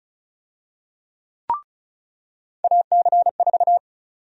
Morse code, heard as a keyed tone, about 36 wpm, 700 Hz. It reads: AC4
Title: USA Call Sign Prefixes - 35wpm